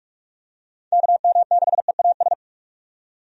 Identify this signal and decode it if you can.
Morse code, heard as a keyed tone, about 45 wpm, 700 Hz. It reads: KM6EAS
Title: USA Call Signs - 6 Characters - 45wpm